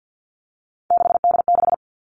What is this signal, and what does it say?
Morse code, heard as a keyed tone, about 50 wpm, 700 Hz. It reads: 6D6
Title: All Cut Numbers - 3-Digit 50wpm